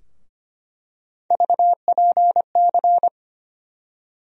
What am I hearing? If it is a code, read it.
Morse code, heard as a keyed tone, about 25 wpm, 700 Hz. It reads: VPC
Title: USA Call Sign Postfixes - 3 Letters 25wpm